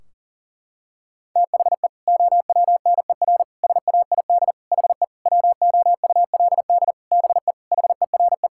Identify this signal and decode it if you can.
Morse code, heard as a keyed tone, about 40 wpm, 700 Hz. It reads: THE OWNER SAID HE WOULD BE HERE
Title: Sentences from Top 1600 Words 40wpm